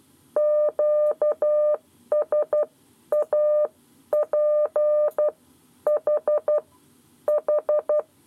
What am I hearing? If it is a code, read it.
Morse code, heard as a keyed tone, about 12 wpm, spaced as for 10 wpm, 600 Hz. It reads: QSAP H H
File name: 7 Lezione cw 1